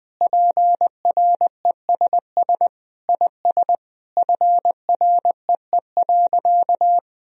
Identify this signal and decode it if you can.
Morse code, heard as a keyed tone, about 20 wpm, 700 Hz. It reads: PRESS IS FREE.
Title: Sentences from Top 1000 Words - Review Part 1 - 20wpm